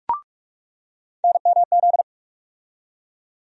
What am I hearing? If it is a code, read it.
Morse code, heard as a keyed tone, about 46 wpm, 700 Hz. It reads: NMZ
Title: USA Call Sign Postfixes - 3 Letters 45wpm